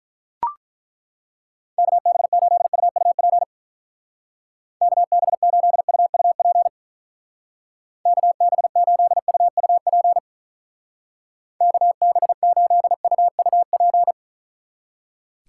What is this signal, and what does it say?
Morse code, heard as a keyed tone, about 42 wpm, 700 Hz. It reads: KB8UUP KB8UUP KB8UUP KB8UUP
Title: Speed Racing - USA 2x3 Call Signs 35wpm